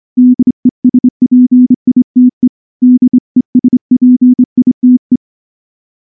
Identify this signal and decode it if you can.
Morse code, heard as a keyed tone, about 24 wpm, 250 Hz. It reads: DESPITE DESPITE
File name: morse-training-25WPM